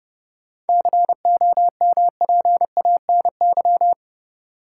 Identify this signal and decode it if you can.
Morse code, heard as a keyed tone, about 30 wpm, 700 Hz. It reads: COMPANY
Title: Sets of 2 Words - New Word Drill - 30wpm